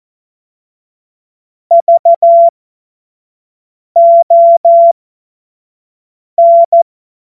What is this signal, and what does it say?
Morse code, heard as a keyed tone, about 14 wpm, spaced as for 5 wpm, 680 Hz. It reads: VON
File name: Morsekurs / Witze / Witz 28. März 2023 25 F70